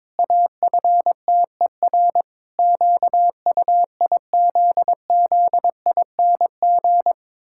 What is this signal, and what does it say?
Morse code, heard as a keyed tone, about 22 wpm, 700 Hz. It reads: AFTER QUIZZING
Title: Pangram Challenge v2 - 22wpm